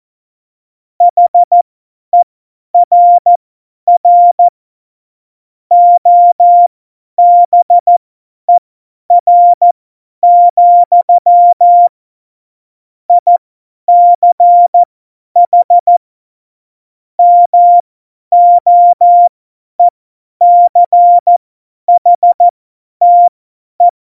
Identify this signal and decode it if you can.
Morse code, heard as a keyed tone, about 14 wpm, spaced as for 10 wpm, 700 Hz. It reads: HERR OBER, ICH MOECHTE
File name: Morsekurs / Witze / Witz 03. September 2019, 50 F70